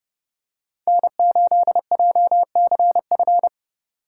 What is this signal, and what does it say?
Morse code, heard as a keyed tone, about 29 wpm, 700 Hz. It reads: N8JCF